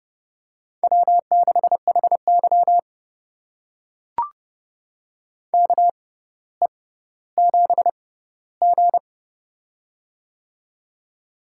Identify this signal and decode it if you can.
Morse code, heard as a keyed tone, about 30 wpm, spaced as for 15 wpm, 700 Hz. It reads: W6HY KE7G
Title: USA Call Signs - 4 Characters - 6x Character Spacing - 30wpm